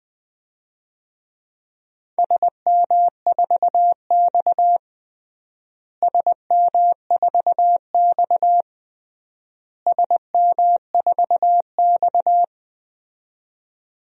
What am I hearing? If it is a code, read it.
Morse code, heard as a keyed tone, about 20 wpm, 700 Hz. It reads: SM4X SM4X SM4X
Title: DXCC-100 Call Signs - 4 Characters - Repeated 3x - 20wpm